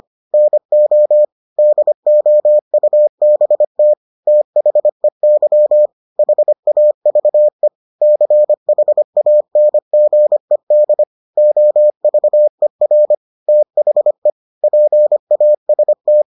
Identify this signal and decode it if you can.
Morse code, heard as a keyed tone, about 25 wpm, 600 Hz. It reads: NO DOUBT THEY HAVE CHANGED OVER THE PAST